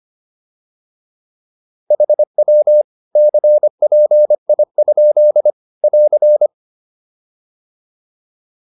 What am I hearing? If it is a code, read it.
Morse code, heard as a keyed tone, about 25 wpm, 600 Hz. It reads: HW CPI? <AR>